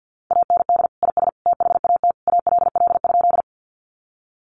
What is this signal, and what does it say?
Morse code, heard as a keyed tone, about 50 wpm, 700 Hz. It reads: AND IS THAT ALL?
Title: Sentences from Top 100 Words 50wpm